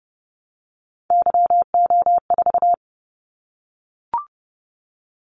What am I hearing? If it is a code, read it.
Morse code, heard as a keyed tone, about 30 wpm, 700 Hz. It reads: YO4
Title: Callsign Prefixes for Top 25 DXCC - 30wpm